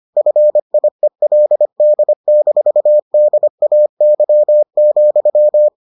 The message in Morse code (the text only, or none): FIELD-DAY,